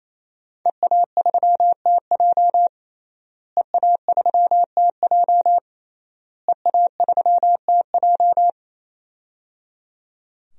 Morse code, heard as a keyed tone, about 27 wpm, 700 Hz. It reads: EA3TJ EA3TJ EA3TJ
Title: DXCC-100 Call Signs - 5 Characters - Repeated 3x - 28wpm